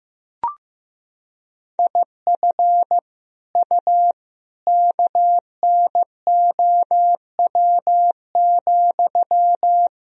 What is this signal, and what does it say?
Morse code, heard as a keyed tone, about 16 wpm, 700 Hz. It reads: IF U KNOW,
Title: Sentences from Top 300 Words and Abbreviations 15wpm